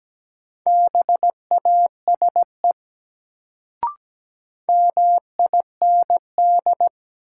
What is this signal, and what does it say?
Morse code, heard as a keyed tone, about 17 wpm, 700 Hz. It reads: BASE MIND
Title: Sets of 5 Words - New Word Drill - 17wpm